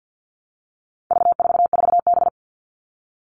Morse code, heard as a keyed tone, about 50 wpm, 700 Hz. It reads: V44B
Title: All Cut Numbers - 4-Digit 50wpm